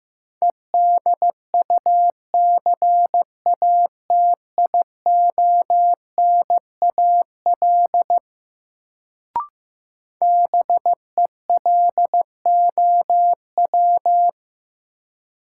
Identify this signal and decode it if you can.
Morse code, heard as a keyed tone, about 15 wpm, 700 Hz. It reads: EDUCATIONAL BELOW
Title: Sentences from Top 1200 Words - New Word Drill Alt - 15wpm